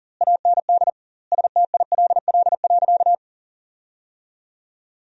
Morse code, heard as a keyed tone, about 40 wpm, 700 Hz. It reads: AND STILL.
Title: 2x Word-Spacing - Sentences from Top 500 Words - 40wpm